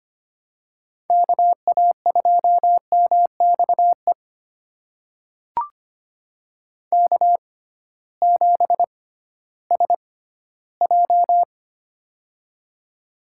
Morse code, heard as a keyed tone, about 25 wpm, spaced as for 14 wpm, 700 Hz. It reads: KA2MXE K7SJ
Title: USA Call Signs - 6x Character Spacing - 25wpm